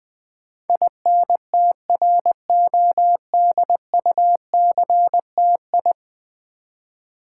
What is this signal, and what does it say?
Morse code, heard as a keyed tone, about 20 wpm, 700 Hz. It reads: INTRODUCTI
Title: Build A Word - 5 Letters At A Time - 12-Letter Words - 20wpm